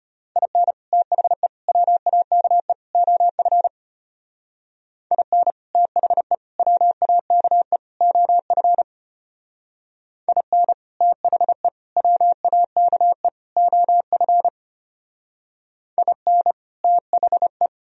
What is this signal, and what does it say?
Morse code, heard as a keyed tone, about 32 wpm, 700 Hz. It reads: IN THE WAKE OF IN THE WAKE OF IN THE WAKE OF IN THE